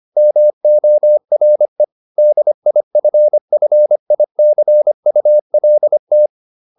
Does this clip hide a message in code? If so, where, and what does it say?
Yes: Morse code from 0.2 to 6.3 s, heard as a keyed tone, about 25 wpm, 600 Hz. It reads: MORE DIFFICULT